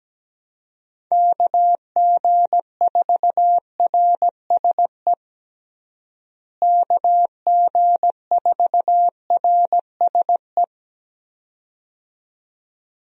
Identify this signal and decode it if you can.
Morse code, heard as a keyed tone, about 17 wpm, 700 Hz. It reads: KG4RSE KG4RSE
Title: USA Call Signs - 6 Characters - Repeated 2x - 17wpm